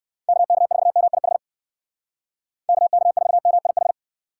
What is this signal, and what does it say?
Morse code, heard as a keyed tone, about 55 wpm, 700 Hz. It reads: KK4MIH KK4MIH
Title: Speed Racing - USA Call Signs 45wpm